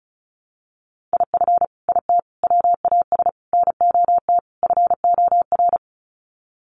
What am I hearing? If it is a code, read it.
Morse code, heard as a keyed tone, about 34 wpm, 700 Hz. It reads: IF IT WAS NOT FOR
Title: Sets of 5 Words Encore 35wpm